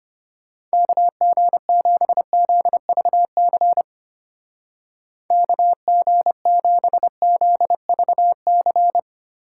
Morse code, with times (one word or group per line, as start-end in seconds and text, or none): 0.7-3.8 KG7ZVC
5.3-9.0 KG7ZVC